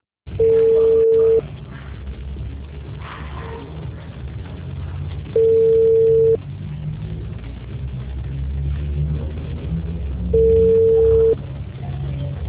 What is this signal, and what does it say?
Ringback tone (China, 450 Hz).
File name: out- -2008- - - .455